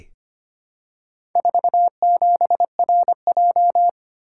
Morse code, heard as a keyed tone, about 25 wpm, 700 Hz. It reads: 47RJ
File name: practice / 25wpm / Four Letter-Number Combinations - 25wpm